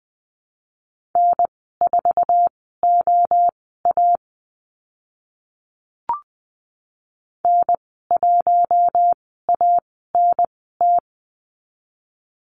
Morse code, heard as a keyed tone, about 20 wpm, spaced as for 14 wpm, 700 Hz. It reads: N4OA N1ANT
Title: 048 Lesson 9g - Number 4 - Call Signs 20wpm